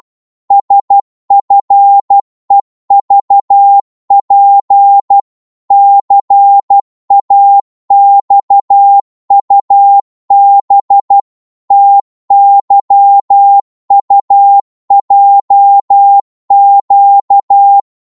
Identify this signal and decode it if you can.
Morse code, heard as a keyed tone, about 12 wpm, 800 Hz. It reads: SFEVP CAXUB TYUJQ
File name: Morse / ltr-12wpm